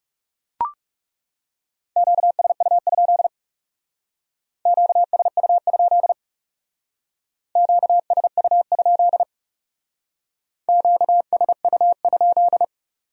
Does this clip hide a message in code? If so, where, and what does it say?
Yes: Morse code from 2.0 to 12.6 s, heard as a keyed tone, about 39 wpm, 700 Hz. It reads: QSU? QSU? QSU? QSU?